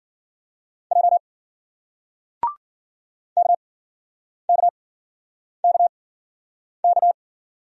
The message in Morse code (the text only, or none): J KKKK